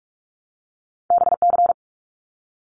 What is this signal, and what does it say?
Morse code, heard as a keyed tone, about 46 wpm, 700 Hz. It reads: BC
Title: Canadian Provinces and Territories 45wpm